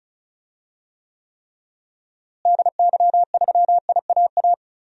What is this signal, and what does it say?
Morse code, heard as a keyed tone, about 34 wpm, 700 Hz. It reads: DY3IAA
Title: DXCC-100 Call Signs - 6 Characters - Repeated 3x - 35wpm